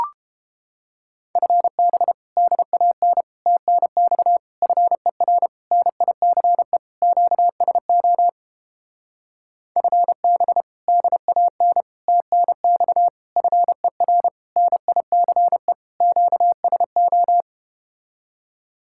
Morse code, heard as a keyed tone, about 31 wpm, 700 Hz. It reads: FB DAN TNX FER NICE QSO FB DAN TNX FER NICE QSO